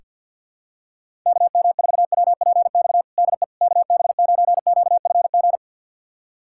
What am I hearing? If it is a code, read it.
Morse code, heard as a keyed tone, about 48 wpm, 700 Hz. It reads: KM4WPX DE KB9QFG
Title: USA Callsign DE Callsign - 6 Characters - 50wpm